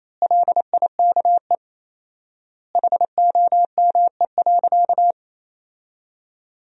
Morse code, heard as a keyed tone, about 27 wpm, 700 Hz. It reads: LIKE HOME.
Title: 4x Word-Spacing - Sentences from Top 200 Words - 28wpm